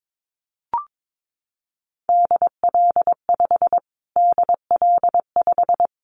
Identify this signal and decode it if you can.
Morse code, heard as a keyed tone, about 22 wpm, 700 Hz. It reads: DL5 DL5
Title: Callsign Prefixes for Top 25 DXCC - Repeated 3x - 22wpm